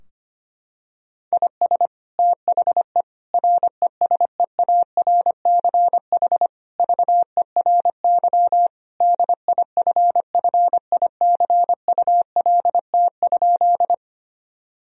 Morse code, heard as a keyed tone, about 25 wpm, 700 Hz. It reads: IS THE RESEARCH VERY DIFFICULT?
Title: Sentences from Top 1000 Words - Review Part 4 - 25wpm